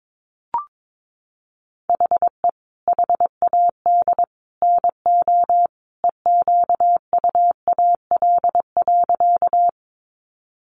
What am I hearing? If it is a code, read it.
Morse code, heard as a keyed tone, about 22 wpm, 700 Hz. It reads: HE HAD NO EQUAL.
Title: Sentences from Top 1400 Words 22wpm